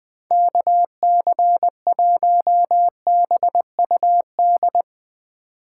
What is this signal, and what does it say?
Morse code, heard as a keyed tone, about 21 wpm, 700 Hz. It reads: KC1BUD
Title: USA Call Signs - 6 Characters - 6x Character Spacing - 20wpm